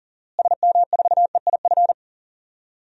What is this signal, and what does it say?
Morse code, heard as a keyed tone, about 40 wpm, 700 Hz. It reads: SM4EIF
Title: DXCC-100 Call Signs - 6 Characters - Repeated 3x - 40wpm